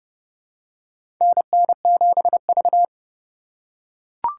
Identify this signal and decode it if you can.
Morse code, heard as a keyed tone, about 30 wpm, 700 Hz. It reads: NN7V
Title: USA Call Signs - 4x Character Spacing - 30wpm